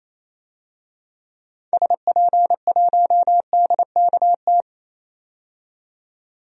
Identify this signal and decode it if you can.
Morse code, heard as a keyed tone, about 28 wpm, 700 Hz. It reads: SP1DKT
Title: DXCC-100 Call Signs - 6 Characters - 28wpm